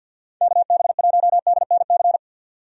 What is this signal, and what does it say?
Morse code, heard as a keyed tone, about 52 wpm, 700 Hz. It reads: KB1DNC
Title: USA Call Signs - 6 Characters - 50wpm